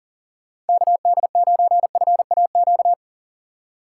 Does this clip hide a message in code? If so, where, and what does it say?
Yes: Morse code from 0.7 to 2.9 s, heard as a keyed tone, about 50 wpm, 700 Hz. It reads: KD9FAQ